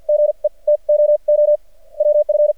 Morse code, heard as a keyed tone, about 48 wpm, spaced as for 29 wpm, 600 Hz. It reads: GET OOWF
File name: archived / CRSnet